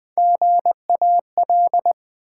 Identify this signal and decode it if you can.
Morse code, heard as a keyed tone, about 20 wpm, 700 Hz. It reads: GAL
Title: Build A Word - 3 Letters At A Time - 6-Letter Words - 20wpm